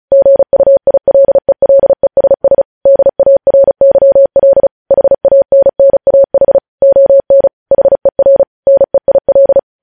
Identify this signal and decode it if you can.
Morse code, heard as a keyed tone, about 36 wpm, 550 Hz. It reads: GUILELESS DARYL HANNAH ON HER NEIL
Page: news headlines in morse code at 35 wpm